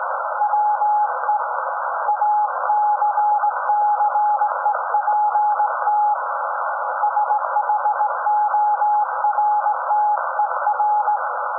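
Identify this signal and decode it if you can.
Morse code, heard as a keyed tone, about 18 wpm, 840 Hz, with noise. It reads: WE ACCEPT SHORTER
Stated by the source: SAQ Grimeton Radio, Alexanderson 200 kW alternator on 17.2 kHz, received in Potsdam with a very strong signal on World Radio Day.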